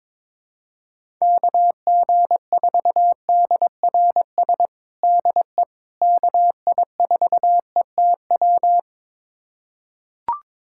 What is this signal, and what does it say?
Morse code, heard as a keyed tone, about 21 wpm, 700 Hz. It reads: KG4DRS DE KI4ETW
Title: USA Callsign DE Callsign - 6 Characters - 22wpm